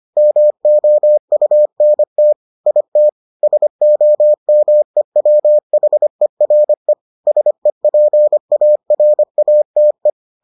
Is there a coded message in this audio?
Yes: Morse code from 0.2 to 10.1 s, heard as a keyed tone, about 25 wpm, 600 Hz. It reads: MOUNT IT SOMEWHERE SEPARATE